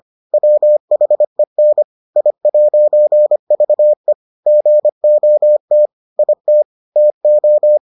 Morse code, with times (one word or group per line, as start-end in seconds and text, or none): 0.3-1.8 WHEN
2.2-4.1 I'VE
4.5-5.9 GOT
6.2-6.6 IT
7.0-7.8 TO